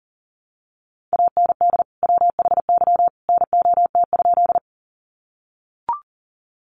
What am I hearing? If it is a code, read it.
Morse code, heard as a keyed tone, about 40 wpm, 700 Hz. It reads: AND WHY NOT?
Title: Sentences from Top 600 Words 40wpm